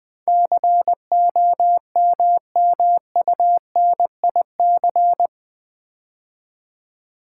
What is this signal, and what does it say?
Morse code, heard as a keyed tone, about 20 wpm, 700 Hz. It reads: COMMUNIC
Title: Build A Word - 4 Letters At A Time - 11-Letter Words - 20wpm